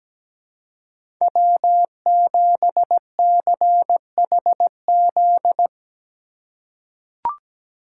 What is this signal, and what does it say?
Morse code, heard as a keyed tone, about 17 wpm, 700 Hz. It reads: W7CHZ
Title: USA Call Signs - 5 Characters - 4x Character Spacing - 17wpm